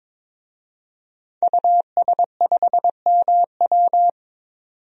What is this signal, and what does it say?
Morse code, heard as a keyed tone, about 22 wpm, 700 Hz. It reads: US5MW
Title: DXCC-100 Call Signs - 5 Characters - 22wpm